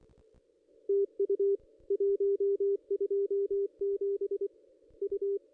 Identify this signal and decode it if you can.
Morse code, heard as a keyed tone, about 24 wpm, 400 Hz, with radio noise. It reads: TU 127 U
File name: contests / ss-cw-k5tr-op-n5rz -1600